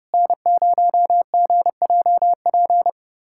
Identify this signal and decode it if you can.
Morse code, heard as a keyed tone, about 30 wpm, 690 Hz. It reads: N0GJP
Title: SOTA QSO Practice 30wpm